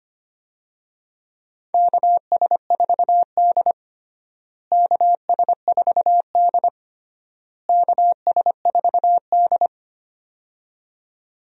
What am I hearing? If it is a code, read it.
Morse code, heard as a keyed tone, about 25 wpm, 700 Hz. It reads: KS4D KS4D KS4D
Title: USA Call Signs - 4 Characters - Repeated 3x - 25wpm